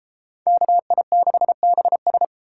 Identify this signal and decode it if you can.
Morse code, heard as a keyed tone, about 33 wpm, 700 Hz. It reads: KI6BS